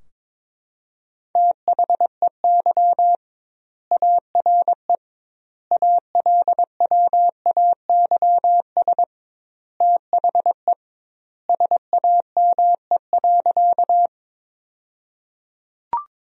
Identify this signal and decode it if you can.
Morse code, heard as a keyed tone, about 22 wpm, 700 Hz. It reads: THEY ARE ALWAYS THE SAME.
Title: 2x Word-Spacing - Sentences from Top 300 Words - 22wpm